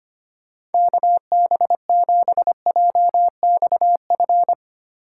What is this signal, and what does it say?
Morse code, heard as a keyed tone, about 32 wpm, 700 Hz. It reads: KB7JXF